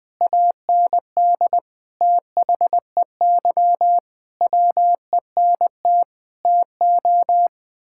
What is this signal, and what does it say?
Morse code, heard as a keyed tone, about 21 wpm, 700 Hz. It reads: AND THEY WENT TO
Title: Sentences from Top 1600 Words 20wpm